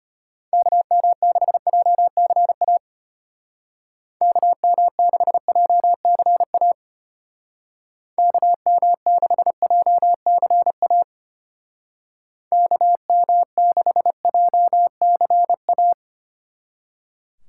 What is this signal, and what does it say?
Morse code, heard as a keyed tone, about 30 wpm, 700 Hz. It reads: KM6JCA KM6JCA KM6JCA KM6JCA
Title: Speed Racing - USA Call Signs 25wpm